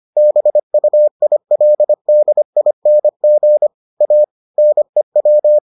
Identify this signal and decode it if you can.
Morse code, heard as a keyed tone, about 25 wpm, 600 Hz. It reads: BUILDING A NEW